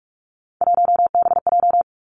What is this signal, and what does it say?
Morse code, heard as a keyed tone, about 46 wpm, 700 Hz. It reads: 1BJ
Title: Three Challenging Letter-Number Combinations 45wpm